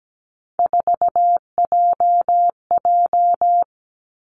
Morse code, heard as a keyed tone, about 17 wpm, 700 Hz. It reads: 4JJ
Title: Three Challenging Letter-Number Combinations 17wpm